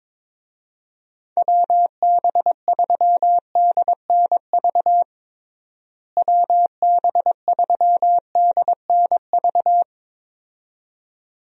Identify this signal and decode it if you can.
Morse code, heard as a keyed tone, about 22 wpm, 700 Hz. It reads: WB3DNV WB3DNV
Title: USA Call Signs - 6 Characters - Repeated 2x - 22wpm